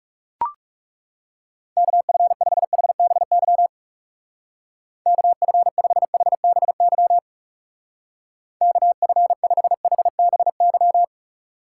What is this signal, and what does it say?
Morse code, heard as a keyed tone, about 34 wpm, 700 Hz. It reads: KF5HBY KF5HBY KF5HBY
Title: Speed Racing - USA Call Signs 30wpm